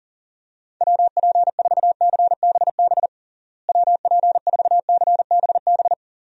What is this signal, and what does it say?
Morse code, heard as a keyed tone, about 40 wpm, 700 Hz. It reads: WP4CBB WP4CBB
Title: USA Call Signs - 6 Characters - Repeated 3x - 40wpm